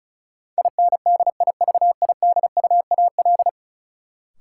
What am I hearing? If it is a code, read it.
Morse code, heard as a keyed tone, about 36 wpm, 700 Hz. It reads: INDIVIDUAL
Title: Sentences from Top 1000 Words - Review Part 4 - New Word Drill - 35wpm